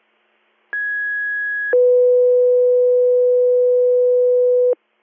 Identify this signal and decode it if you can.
Two-tone page, heard as a one-second tone, then a three-second tone, about 1650 then 500 Hz. It is heard from 0.7 to 4.7 s.